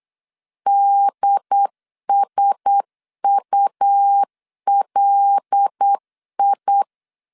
Morse code, heard as a keyed tone, about 8 wpm, 800 Hz. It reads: DSULI
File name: CW / DET / 1 6 1500C S B 8wpm 5142